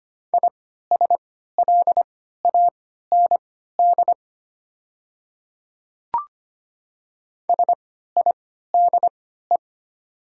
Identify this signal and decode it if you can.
Morse code, heard as a keyed tone, about 25 wpm, spaced as for 14 wpm, 700 Hz. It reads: ISLAND SIDE
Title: 3x Character Spacing - Top 500 Words - 25wpm